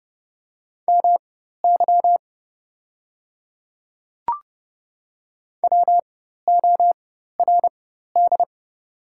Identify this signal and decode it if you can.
Morse code, heard as a keyed tone, about 30 wpm, spaced as for 14 wpm, 690 Hz. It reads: MY WORD
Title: 4x Character Spacing - Top 100 Words - 30wpm